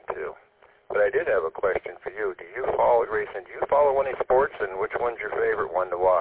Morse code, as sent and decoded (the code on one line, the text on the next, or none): none